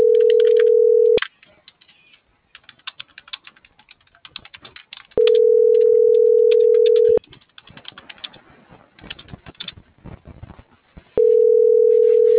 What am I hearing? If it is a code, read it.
Ringback tone (North America), 440 + 480 Hz.